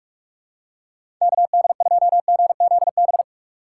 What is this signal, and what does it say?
Morse code, heard as a keyed tone, about 47 wpm, 700 Hz. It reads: KD2GZB
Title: USA Call Signs - 6 Characters - 2x Character Spacing - 45wpm